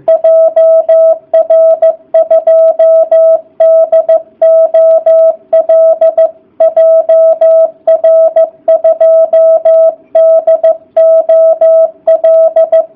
Morse code, heard as a keyed tone, about 15 wpm, 660 Hz. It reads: JR2DOL JR2DOL